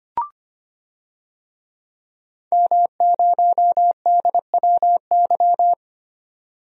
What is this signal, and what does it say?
Morse code, heard as a keyed tone, about 25 wpm, 700 Hz. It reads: M0DWY
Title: DXCC-100 Call Signs - 5 Characters - 25wpm